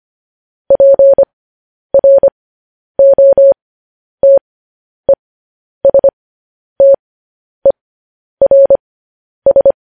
Morse code, heard as a keyed tone, about 25 wpm, spaced as for 10 wpm, 550 Hz. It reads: PROTESTERS
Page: News Headlines in Morse Code at 10 WPM